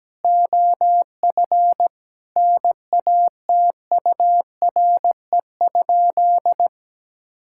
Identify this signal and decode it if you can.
Morse code, heard as a keyed tone, about 17 wpm, 700 Hz. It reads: OF NATURE?
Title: Sentences from Top 1000 Words - Review Part 5 - 17wpm